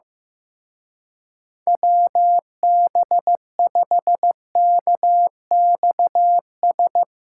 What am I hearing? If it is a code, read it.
Morse code, heard as a keyed tone, about 15 wpm, 700 Hz. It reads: WB5KXS
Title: Speed Racing - USA Call Signs 15wpm